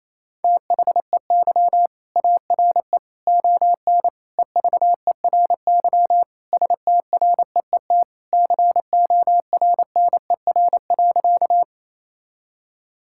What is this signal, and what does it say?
Morse code, heard as a keyed tone, about 28 wpm, 700 Hz. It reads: THEY ARE ON EVERY STREET CORNER.
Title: Sentences from Top 1600 Words 28wpm